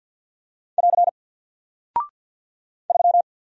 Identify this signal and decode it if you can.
Morse code, heard as a keyed tone, about 49 wpm, 700 Hz. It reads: <AR> 3
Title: Single Character (Letter:Number:Punctuation:Prosign) Alt 50wpm